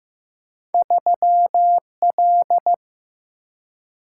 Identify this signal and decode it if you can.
Morse code, heard as a keyed tone, about 15 wpm, 700 Hz. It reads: 3L
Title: Single-Letter Single-Number 15wpm